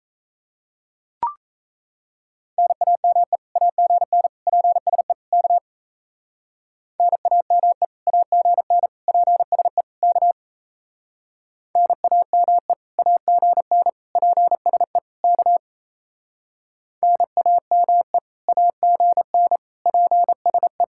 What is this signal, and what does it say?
Morse code, heard as a keyed tone, about 36 wpm, 700 Hz. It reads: NAME AGN PSE K NAME AGN PSE K NAME AGN PSE K NAME AGN PSE